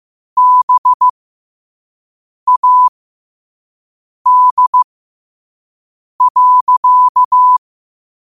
Morse code, heard as a keyed tone, about 15 wpm, spaced as for 5 wpm, 1000 Hz. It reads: BAD.
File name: qotdfiles / 5wpm